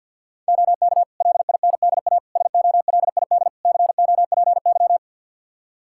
Morse code, heard as a keyed tone, about 51 wpm, 700 Hz. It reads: OK LINDA SOLID COPY
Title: Practice QSO - Exchange 3 - More Info - 50wpm